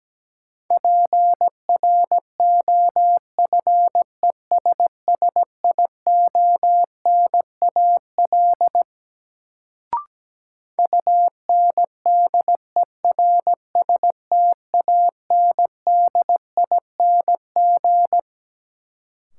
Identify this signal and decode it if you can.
Morse code, heard as a keyed tone, about 17 wpm, 700 Hz. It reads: PROFESSIONAL UNDERSTANDING
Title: Sentences from Top 1400 Words - New Word Drill Alt - 17wpm